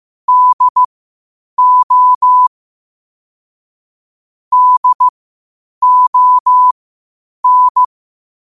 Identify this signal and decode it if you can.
Morse code, heard as a keyed tone, about 15 wpm, spaced as for 8 wpm, 1000 Hz. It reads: DO DON
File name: qotdfiles / 8wpm